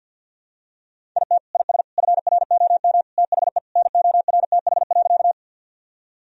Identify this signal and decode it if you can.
Morse code, heard as a keyed tone, about 50 wpm, 700 Hz. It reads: IT IS FROM THE NORTH.
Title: Sentences from Top 1000 Words - Review Part 1 - 50wpm